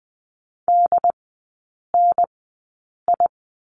Morse code, heard as a keyed tone, about 20 wpm, spaced as for 8 wpm, 700 Hz. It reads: DNI
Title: 184 Lesson 30e - Letter b - Three Characters 20wpm